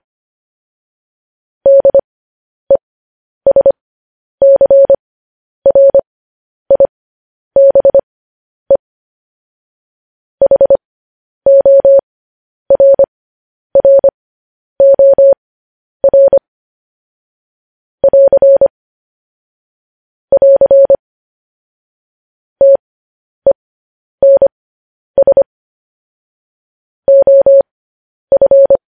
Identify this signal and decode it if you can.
Morse code, heard as a keyed tone, about 25 wpm, spaced as for 10 wpm, 550 Hz. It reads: DESCRIBE HORROR <AR> <AR> TENS OF
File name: podcast / News headlines-1x-25x10